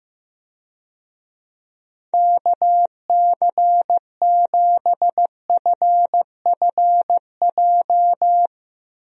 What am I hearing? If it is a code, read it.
Morse code, heard as a keyed tone, about 15 wpm, 700 Hz. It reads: KC7FFJ
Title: USA Call Signs - 6 Characters - Repeated 2x - 15wpm